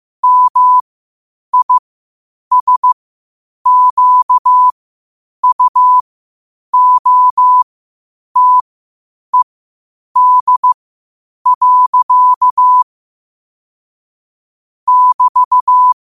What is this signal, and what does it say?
Morse code, heard as a keyed tone, about 15 wpm, spaced as for 8 wpm, 1000 Hz. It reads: MISQUOTED. <BT>
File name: qotdfiles / 8wpm